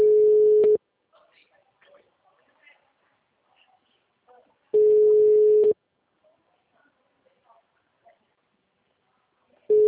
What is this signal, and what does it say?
Ringback tone (Europe, 425 Hz), with speech.
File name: out- -2016- - - .51118